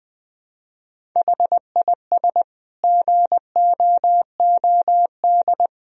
Morse code, heard as a keyed tone, about 21 wpm, 700 Hz. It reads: HIS GOOD
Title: Sentences from Top 1000 Words - Review Part 1 - 20wpm